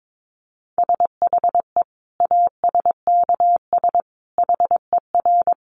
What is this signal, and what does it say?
Morse code, heard as a keyed tone, about 22 wpm, 700 Hz. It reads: SHE ASKS HER
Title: Easy Phrases 22wpm